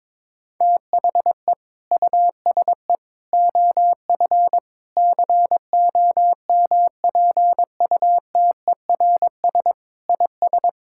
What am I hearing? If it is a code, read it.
Morse code, heard as a keyed tone, about 22 wpm, 700 Hz. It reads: THE USE OF COMPUTERS IS
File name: practice / 22wpm / Sentences with 3-Word Phrases - 22wpm